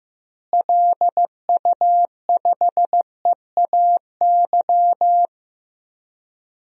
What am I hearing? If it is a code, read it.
Morse code, heard as a keyed tone, about 16 wpm, 700 Hz. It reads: LU5EAY
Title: DXCC-100 Call Signs - 6 Characters - Repeated 3x - 15wpm